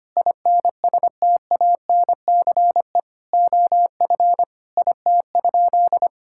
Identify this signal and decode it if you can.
Morse code, heard as a keyed tone, about 25 wpm, 700 Hz. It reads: INSTANCE OF IT?
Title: Sentences from Top 1000 Words - Review Part 5 - 25wpm